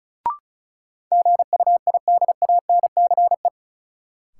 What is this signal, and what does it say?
Morse code, heard as a keyed tone, about 36 wpm, 700 Hz. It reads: GUIDANCE
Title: Sentences from Top 1600 Words - New Word Drill - 35wpm